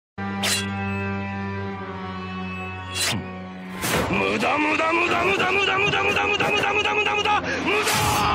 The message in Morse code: none